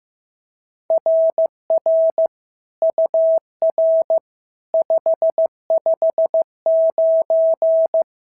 Morse code, heard as a keyed tone, about 15 wpm, 650 Hz. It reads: RR UR 559
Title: SOTA QSO Practice 15wpm